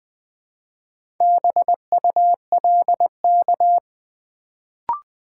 Morse code, heard as a keyed tone, about 20 wpm, 700 Hz. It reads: BULK
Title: Build A Word - 1 Letter At A Time - 4-Letter Words - 20wpm